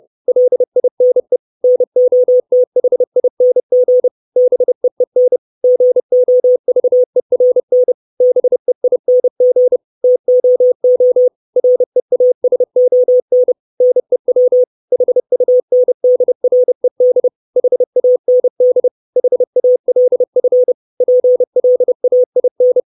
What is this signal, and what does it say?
Morse code, heard as a keyed tone, about 30 wpm, 500 Hz. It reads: LINE NOTHING BEEN GOVERN BEING TOO REASON NEW HUNDRED HAND HALF PLAIN
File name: 632 Words 30WPM